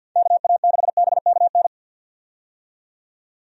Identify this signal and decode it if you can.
Morse code, heard as a keyed tone, about 52 wpm, 700 Hz. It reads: KA6BKN
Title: USA Call Signs - 6 Characters - 50wpm